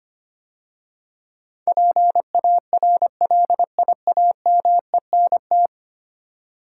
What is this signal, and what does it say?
Morse code, heard as a keyed tone, about 25 wpm, 700 Hz. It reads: PARLIAMENT
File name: 10 Letter Words - First 3 Letters Given - 25wpm